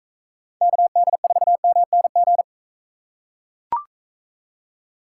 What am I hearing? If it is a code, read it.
Morse code, heard as a keyed tone, about 43 wpm, 700 Hz. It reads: KD4MNG